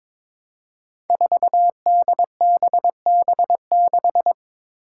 Morse code, heard as a keyed tone, about 22 wpm, 700 Hz. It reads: 4DBB6
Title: Five Challenging Letter-Number Combinations 22wpm